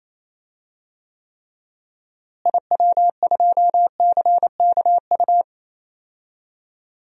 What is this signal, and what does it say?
Morse code, heard as a keyed tone, about 28 wpm, 700 Hz. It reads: IW2CKU
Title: DXCC-100 Call Signs - 6 Characters - Repeated 2x - 28wpm